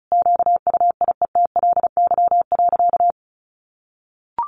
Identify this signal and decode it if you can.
Morse code, heard as a keyed tone, about 34 wpm, 700 Hz. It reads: QUIETLY.
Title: A Sound Of Thunder 35wpm